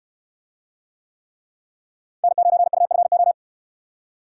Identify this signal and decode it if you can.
Morse code, heard as a keyed tone, about 69 wpm, 700 Hz. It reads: N9UFO